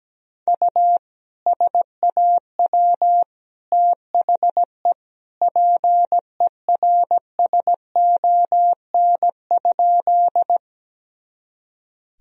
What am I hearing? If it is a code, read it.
Morse code, heard as a keyed tone, about 17 wpm, 700 Hz. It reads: U SAW THE PERSON?
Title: Sentences from Top 400 Words and Abbreviations 17wpm